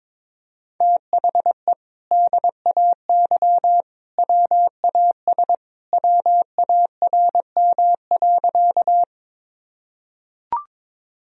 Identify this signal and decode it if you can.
Morse code, heard as a keyed tone, about 22 wpm, 700 Hz. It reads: THE DAY WAS WARM.
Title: Sentences from Top 1000 Words - Review Part 2 - 22wpm